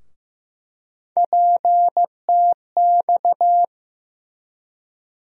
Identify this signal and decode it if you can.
Morse code, heard as a keyed tone, about 15 wpm, 700 Hz. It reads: PTX
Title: USA Call Sign Postfixes - 3 Letters 15wpm